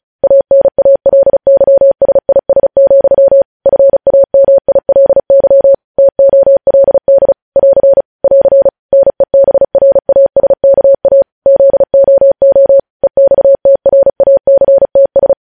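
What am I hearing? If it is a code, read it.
Morse code, heard as a keyed tone, about 36 wpm, 550 Hz. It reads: ANALYSIS, FAMILY TOLD <AR> <AR> NEBRASKA ZOO EXTRACTS
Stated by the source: Morse news headlines sent at 35 words per minute.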